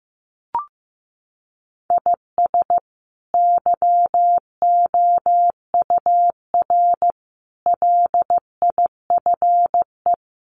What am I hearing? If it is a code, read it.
Morse code, heard as a keyed tone, about 15 wpm, 700 Hz. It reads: IS YOUR LIFE
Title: Sentences from Top 1500 Words 15wpm